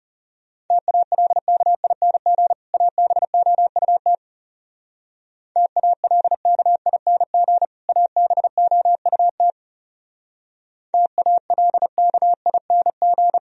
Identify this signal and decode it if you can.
Morse code, heard as a keyed tone, about 37 wpm, 700 Hz. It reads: TALKING ABOUT TALKING ABOUT TALKING
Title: Speed-Racing - Sets of 2 Words 30wpm